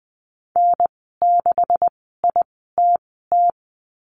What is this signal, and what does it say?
Morse code, heard as a keyed tone, about 20 wpm, spaced as for 15 wpm, 700 Hz. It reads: N6ITT
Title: 128 Lesson 21g - Number 6 - Call Signs 20wpm